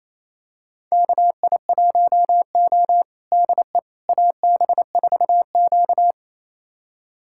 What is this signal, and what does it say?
Morse code, heard as a keyed tone, about 28 wpm, 700 Hz. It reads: KI1O DE AB4Q
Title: USA Callsign DE Callsign - 4 Characters - 28wpm